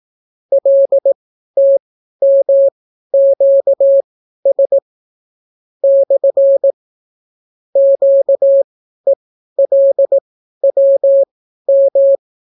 Morse code, heard as a keyed tone, about 18 wpm, spaced as for 12 wpm, 550 Hz. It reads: LTMQS / QELWM